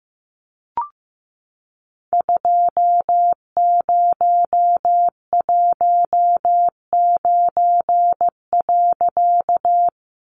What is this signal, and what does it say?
Morse code, heard as a keyed tone, about 15 wpm, 700 Hz. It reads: 2019.
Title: A Sound Of Thunder 15wpm